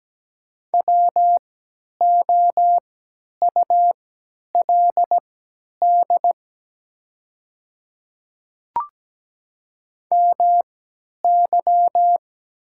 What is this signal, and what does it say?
Morse code, heard as a keyed tone, about 17 wpm, spaced as for 10 wpm, 700 Hz. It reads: WOULD MY
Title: 3x Character Spacing - Top 100 Words - 17wpm